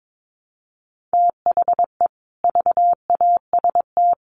Morse code, heard as a keyed tone, about 22 wpm, 700 Hz. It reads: THE VAST
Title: Sets of 5 Words Encore 22wpm